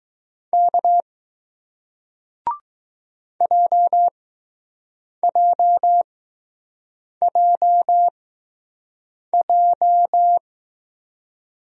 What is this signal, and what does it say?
Morse code, heard as a keyed tone, about 20 wpm, spaced as for 6 wpm, 700 Hz. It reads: K JJJJ